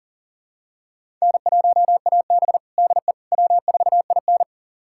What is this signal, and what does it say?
Morse code, heard as a keyed tone, about 40 wpm, 700 Hz. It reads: N1AB DE W4IN
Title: USA Callsign DE Callsign - 4 Characters - 40wpm